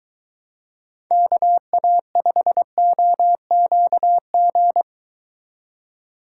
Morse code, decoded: KA5OQG

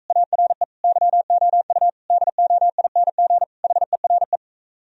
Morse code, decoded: ARE YOU DOING HERE

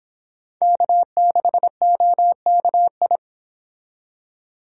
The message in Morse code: K6OKI

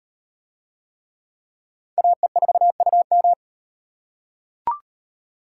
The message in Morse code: AE4UM

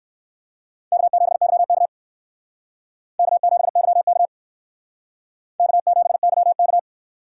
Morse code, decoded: K7YK K7YK K7YK

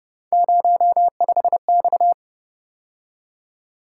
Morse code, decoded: 05X

